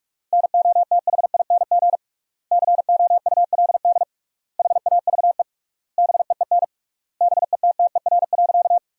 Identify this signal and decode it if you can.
Morse code, heard as a keyed tone, about 48 wpm, 700 Hz. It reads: NOTHING COULD HAVE BEEN BETTER.
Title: 3x Word-Spacing - Sentences from Top 500 Words - 45wpm